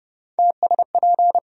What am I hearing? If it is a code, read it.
Morse code, heard as a keyed tone, about 30 wpm, 700 Hz. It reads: TSP
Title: USA Call Sign Postfixes - 3 Letters 30wpm